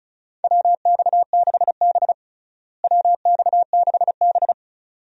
Morse code, heard as a keyed tone, about 36 wpm, 700 Hz. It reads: WX6B WX6B
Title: USA Call Signs - 4 Characters - Repeated 2x - 35wpm